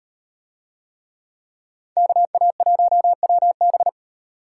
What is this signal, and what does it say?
Morse code, heard as a keyed tone, about 34 wpm, 700 Hz. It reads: KA1WB